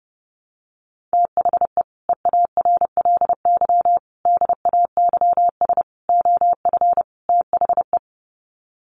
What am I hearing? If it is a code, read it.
Morse code, heard as a keyed tone, about 30 wpm, 700 Hz. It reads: THE EARLY DAYS OF THE
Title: Sets of 5 Words Encore 30wpm